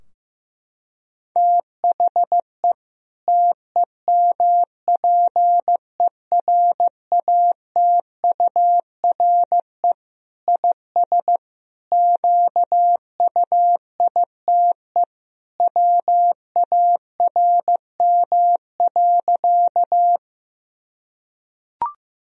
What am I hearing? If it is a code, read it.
Morse code, heard as a keyed tone, about 15 wpm, 700 Hz. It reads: THE TEMPERATURE IS QUITE WARM.
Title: Sentences from Top 1000 Words - Review Part 4 - 15wpm